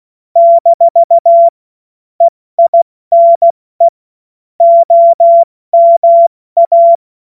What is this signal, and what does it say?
Morse code, heard as a keyed tone, about 16 wpm, spaced as for 14 wpm, 680 Hz. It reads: - EINE OMA
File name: Morsekurs / Witze / Witz 31. August 2021, 70 F80